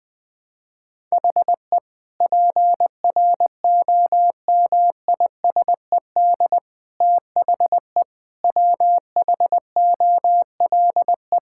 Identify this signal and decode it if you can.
Morse code, heard as a keyed tone, about 20 wpm, 700 Hz. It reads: HE PROMISED THE WHOLE
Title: Sentences with 8-Word Phrases - Spoken Beginning - 20wpm